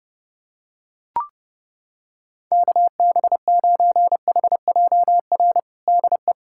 Morse code, heard as a keyed tone, about 32 wpm, 700 Hz. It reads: KB9HJR DE
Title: USA Callsign DE Callsign - 6 Characters - 30wpm